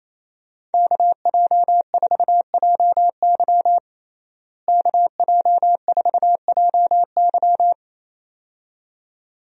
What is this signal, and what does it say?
Morse code, heard as a keyed tone, about 28 wpm, 700 Hz. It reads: KJ4JY KJ4JY
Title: USA Call Signs - 5 Characters - Repeated 2x - 28wpm